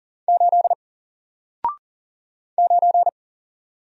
Morse code, heard as a keyed tone, about 39 wpm, 700 Hz. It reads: 8 9
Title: Single Letter-Number Alt 40wpm